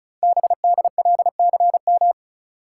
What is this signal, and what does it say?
Morse code, heard as a keyed tone, about 36 wpm, 700 Hz. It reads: BDLCM
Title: Code Groups - 5 Letters - 35wpm